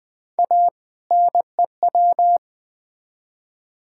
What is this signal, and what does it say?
Morse code, heard as a keyed tone, about 20 wpm, 700 Hz. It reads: A NEW
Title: Build A Phrase - 2-Word Phrases - 20wpm